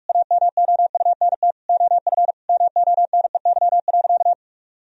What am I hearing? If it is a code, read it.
Morse code, heard as a keyed tone, about 41 wpm, 700 Hz. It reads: AMOUNT OF MONEY.
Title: Sentences from Top 1000 Words - Review Part 4 - 45wpm